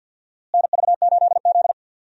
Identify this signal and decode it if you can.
Morse code, heard as a keyed tone, about 51 wpm, 700 Hz. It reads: NV8Z
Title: USA Call Signs - 4 Characters - Repeated 3x - 50wpm